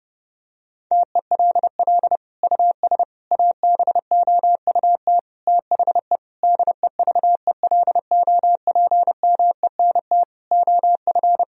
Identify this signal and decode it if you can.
Morse code, heard as a keyed tone, about 32 wpm, 700 Hz. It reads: TELL US ABOUT THE DEVELOPMENT OF
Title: Sentences from Top 1000 Words - Review Part 4 - 30wpm